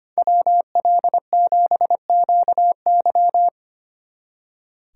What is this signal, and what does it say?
Morse code, heard as a keyed tone, about 30 wpm, 700 Hz. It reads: WL7QY
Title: Speed Racing - USA 2x2 Call Signs 25wpm